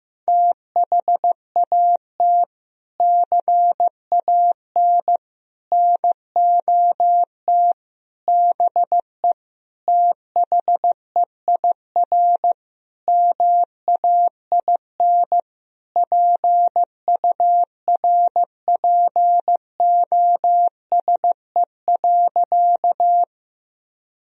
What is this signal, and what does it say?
Morse code, heard as a keyed tone, about 15 wpm, 700 Hz. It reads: THAT CAN NOT BE THEIR MAIN PURPOSE.
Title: Sentences from Top 1000 Words - Review Part 2 - 15wpm